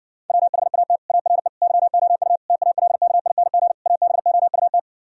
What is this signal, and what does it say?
Morse code, heard as a keyed tone, about 60 wpm, 700 Hz. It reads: WHAT ARE YOU TALKING ABOUT